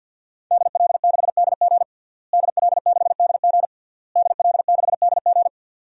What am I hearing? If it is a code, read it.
Morse code, heard as a keyed tone, about 51 wpm, 700 Hz. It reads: DL6DG DL6DG DL6DG
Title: DXCC-100 Call Signs - 5 Characters - Repeated 3x - 50wpm